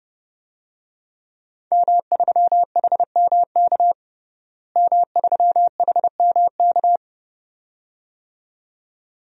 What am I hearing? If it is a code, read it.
Morse code, heard as a keyed tone, about 30 wpm, 700 Hz. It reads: M3HMK M3HMK
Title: DXCC-100 Call Signs - 5 Characters - Repeated 2x - 30wpm